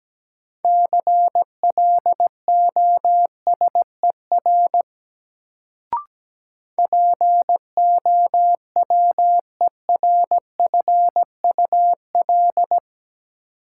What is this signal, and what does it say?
Morse code, heard as a keyed tone, about 17 wpm, 700 Hz. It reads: CLOSER POWERFUL
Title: Sets of 5 Words Encore - New Word Drill - 17wpm